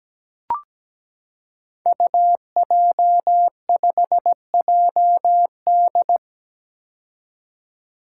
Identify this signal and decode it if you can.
Morse code, heard as a keyed tone, about 17 wpm, 700 Hz. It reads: UJ5JD
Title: Code Groups - 5 Letter:Numbers - 17wpm